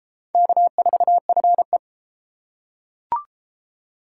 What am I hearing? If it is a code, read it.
Morse code, heard as a keyed tone, about 33 wpm, 700 Hz. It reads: K4FE